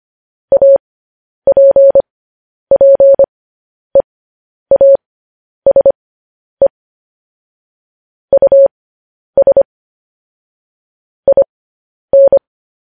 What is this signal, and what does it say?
Morse code, heard as a keyed tone, about 25 wpm, spaced as for 10 wpm, 550 Hz. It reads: APPEASE US IN